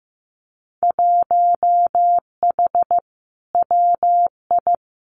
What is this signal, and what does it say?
Morse code, heard as a keyed tone, about 15 wpm, 700 Hz. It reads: 1H WI
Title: Winter Field Day Exchanges 15wpm